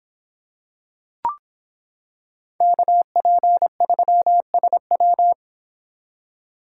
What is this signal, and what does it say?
Morse code, heard as a keyed tone, about 26 wpm, 700 Hz. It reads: KP3SW